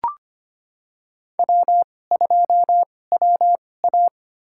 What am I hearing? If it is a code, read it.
Morse code, heard as a keyed tone, about 25 wpm, spaced as for 18 wpm, 700 Hz. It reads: W2WA
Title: USA Call Signs - 4 Characters - 2x Character Spacing - 25wpm